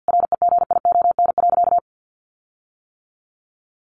Morse code, heard as a keyed tone, about 50 wpm, 700 Hz. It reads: REGION.
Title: Sentences from Top 1500 Words 50wpm